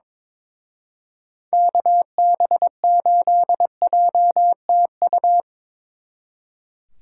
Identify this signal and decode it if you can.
Morse code, heard as a keyed tone, about 25 wpm, 700 Hz. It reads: KB8JTU
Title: Speed Racing - USA 2x3 Call Signs 22wpm